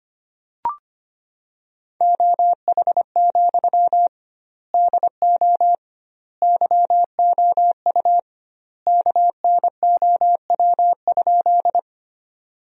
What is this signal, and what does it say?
Morse code, heard as a keyed tone, about 25 wpm, 700 Hz. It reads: OH, DO YOU KNOW?
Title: 2x Word-Spacing - Sentences from Top 500 Words - 25wpm